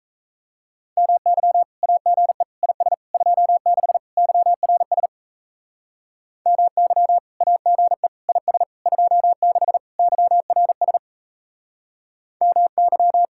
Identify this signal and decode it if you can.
Morse code, heard as a keyed tone, about 34 wpm, 700 Hz. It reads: MY AGE IS 26 YRS MY AGE IS 26 YRS MY
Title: Speed-Racing Common QSO Phrases - 28wpm